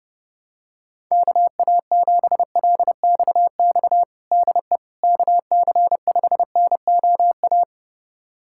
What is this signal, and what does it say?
Morse code, heard as a keyed tone, about 32 wpm, 700 Hz. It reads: KA7LXX DE KC5NOA
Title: USA Callsign DE Callsign 30wpm